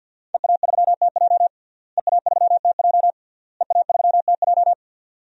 Morse code, heard as a keyed tone, about 49 wpm, 700 Hz. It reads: EA3TJ EA3TJ EA3TJ
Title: DXCC-100 Call Signs - 5 Characters - Repeated 3x - 50wpm